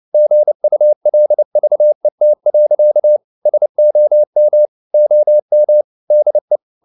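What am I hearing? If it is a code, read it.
Morse code, heard as a keyed tone, about 29 wpm, 600 Hz. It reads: GULVET. SOM OM DE
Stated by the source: Fast (QRQ) Morse at 29 wpm, from CW med Gnister.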